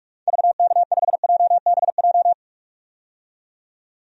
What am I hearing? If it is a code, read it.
Morse code, heard as a keyed tone, about 45 wpm, 700 Hz. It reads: VK5JBJ
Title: DXCC-100 Call Signs - 6 Characters - 45wpm